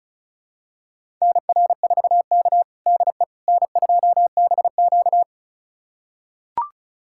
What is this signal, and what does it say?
Morse code, heard as a keyed tone, about 34 wpm, 700 Hz. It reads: NR4K DE N2BQ
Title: USA Callsign DE Callsign - 4 Characters - 35wpm